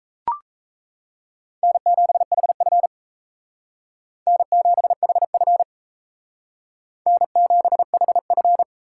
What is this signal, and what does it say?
Morse code, heard as a keyed tone, about 35 wpm, 700 Hz. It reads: N7HF N7HF N7HF
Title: Speed Racing - USA 1x2 Call Signs 28wpm